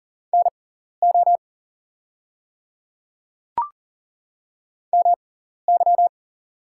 Morse code, heard as a keyed tone, about 40 wpm, spaced as for 14 wpm, 690 Hz. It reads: NO MY
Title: 6x Character Spacing - Top 200 Words - 40wpm